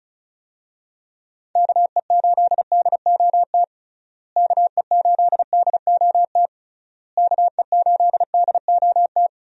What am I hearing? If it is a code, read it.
Morse code, heard as a keyed tone, about 36 wpm, 700 Hz. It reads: KE8DOT KE8DOT KE8DOT
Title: USA Call Signs - 6 Characters - Repeated 3x - 35wpm